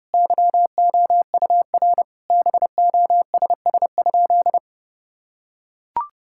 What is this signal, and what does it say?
Morse code, heard as a keyed tone, about 32 wpm, 700 Hz. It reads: YOUR BOSS?
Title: Sentences from Top 1000 Words - Review Part 6 - 30wpm